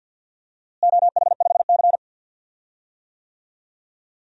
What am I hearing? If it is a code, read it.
Morse code, heard as a keyed tone, about 52 wpm, 700 Hz. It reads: OH5C
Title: DXCC-100 Call Signs - 4 Characters - Repeated 2x - 50wpm